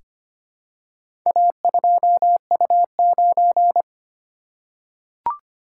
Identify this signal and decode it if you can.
Morse code, heard as a keyed tone, about 25 wpm, 700 Hz. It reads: A2U9
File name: practice / 25wpm / Four Letter-Number Combinations - 25wpm